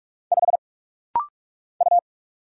Morse code, heard as a keyed tone, about 45 wpm, 700 Hz. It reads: 5 U
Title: Single Letter-Number Alt - Rapid-Fire 45wpm